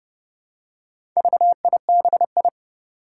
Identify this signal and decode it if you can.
Morse code, heard as a keyed tone, about 30 wpm, 700 Hz. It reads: VIBI
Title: Four Challenging Letter-Number Combinations 30wpm